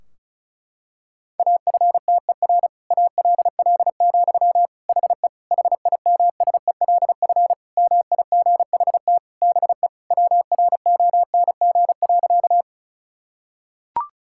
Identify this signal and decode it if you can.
Morse code, heard as a keyed tone, about 36 wpm, 700 Hz. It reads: AFTER ALL, HE HIMSELF MIGHT BE WRONG.
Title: Sentences from Top 1600 Words 35wpm